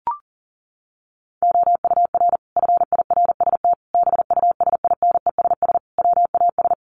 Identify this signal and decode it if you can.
Morse code, heard as a keyed tone, about 40 wpm, 700 Hz. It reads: OUR FIRST BUSINESS WAS
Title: Sentences from Top 1300 Words 40wpm